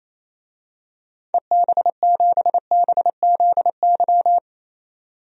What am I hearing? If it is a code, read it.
Morse code, heard as a keyed tone, about 28 wpm, 700 Hz. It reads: EB7BZY
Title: DXCC-100 Call Signs - 6 Characters - 28wpm